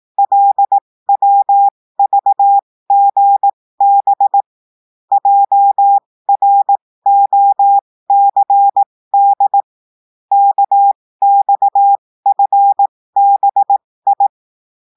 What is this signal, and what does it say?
Morse code, heard as a keyed tone, about 18 wpm, spaced as for 15 wpm, 800 Hz. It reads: LWVGB JROCD KXFBI
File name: cours-du-lundi files / cours-1580-texte-chiffre-15-mpm